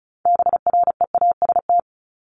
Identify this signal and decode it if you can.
Morse code, heard as a keyed tone, about 34 wpm, 700 Hz. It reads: BREAST
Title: Top 1751-2000 Words 35wpm